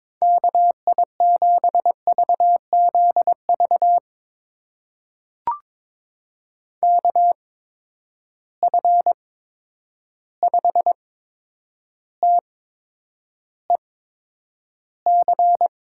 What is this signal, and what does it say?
Morse code, heard as a keyed tone, about 22 wpm, spaced as for 9 wpm, 700 Hz. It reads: KI7VZV KF5TEC
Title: USA Call Signs - 6 Characters - 8x Character Spacing - 22wpm